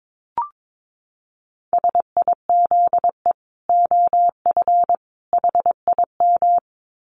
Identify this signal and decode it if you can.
Morse code, heard as a keyed tone, about 22 wpm, 700 Hz. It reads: SIZE OF HIM
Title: Dit Challenge 22wpm